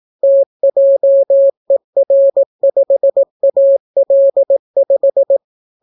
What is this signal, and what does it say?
Morse code, heard as a keyed tone, about 18 wpm, 550 Hz. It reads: TJER5AL5